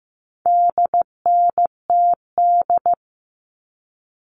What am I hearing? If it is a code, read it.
Morse code, heard as a keyed tone, about 15 wpm, 700 Hz. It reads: DNTD
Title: All Cut Numbers - 4-Digit 15wpm